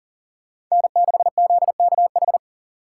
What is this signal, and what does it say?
Morse code, heard as a keyed tone, about 37 wpm, 700 Hz. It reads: N6ZKH
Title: USA Call Signs - 5 Characters - 6x Character Spacing - 40wpm